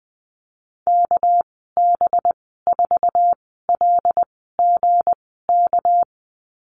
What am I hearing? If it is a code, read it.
Morse code, heard as a keyed tone, about 20 wpm, spaced as for 15 wpm, 700 Hz. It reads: KB4LGK
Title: 238 Lesson 38g - Letter q - Call Signs 20wpm